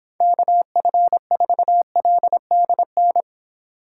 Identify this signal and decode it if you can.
Morse code, heard as a keyed tone, about 25 wpm, 700 Hz. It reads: KF4LDN
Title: Speed Racing - USA 2x3 Call Signs 22wpm